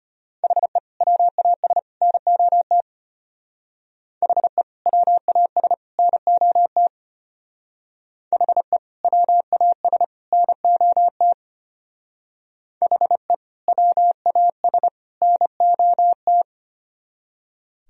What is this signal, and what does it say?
Morse code, heard as a keyed tone, about 33 wpm, 700 Hz. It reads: HE WAS NOT HE WAS NOT HE WAS NOT HE WAS NOT